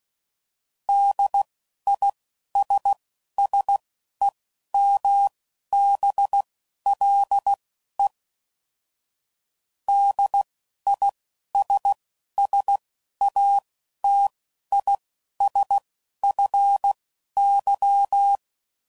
Morse code, heard as a keyed tone, about 16 wpm, spaced as for 11 wpm, 800 Hz. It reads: DISSEMBLE DISSATISFY